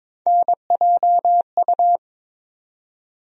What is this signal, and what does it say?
Morse code, heard as a keyed tone, about 22 wpm, 700 Hz. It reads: NJU
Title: USA Call Sign Postfixes - 3 Letters 22wpm